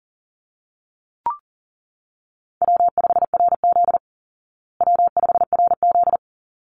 Morse code, heard as a keyed tone, about 40 wpm, 700 Hz. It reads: W5RZ W5RZ
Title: Easy Call Signs Repeated 2x - 40wpm